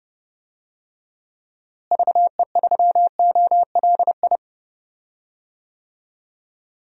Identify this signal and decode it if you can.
Morse code, heard as a keyed tone, about 32 wpm, 700 Hz. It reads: VE3OLI
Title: DXCC-100 Call Signs - 6 Characters - Repeated 2x - 30wpm